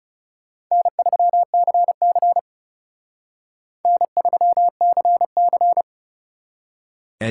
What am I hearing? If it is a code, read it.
Morse code, heard as a keyed tone, about 38 wpm, 700 Hz. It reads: N3CC N3CC